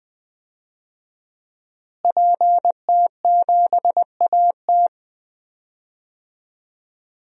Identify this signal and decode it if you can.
Morse code, heard as a keyed tone, about 19 wpm, 700 Hz. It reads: PT7AT
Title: DXCC-100 Call Signs - 5 Characters - Repeated 2x - 20wpm